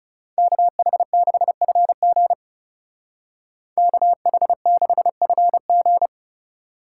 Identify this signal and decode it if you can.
Morse code, heard as a keyed tone, about 35 wpm, 700 Hz. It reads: KH6FG KH6FG
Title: Speed Racing - USA 2x2 Call Signs 30wpm